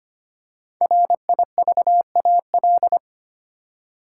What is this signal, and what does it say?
Morse code, heard as a keyed tone, about 25 wpm, 700 Hz. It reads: RIVAL
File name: 5 Letter Words - First 3 Letters Given - 25wpm